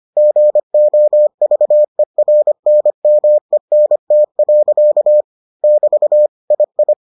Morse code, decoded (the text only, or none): GOVERNMENT. <BT> II